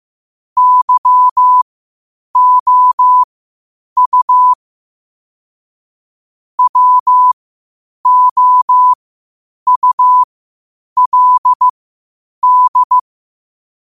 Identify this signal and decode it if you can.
Morse code, heard as a keyed tone, about 15 wpm, spaced as for 8 wpm, 1000 Hz. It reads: YOU WOULD
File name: qotdfiles / 8wpm